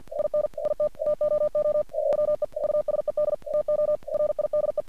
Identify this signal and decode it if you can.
Morse code, heard as a keyed tone, about 45 wpm, 600 Hz. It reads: IN IT TOO EOE USED TO RIDE
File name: archived / CRSnet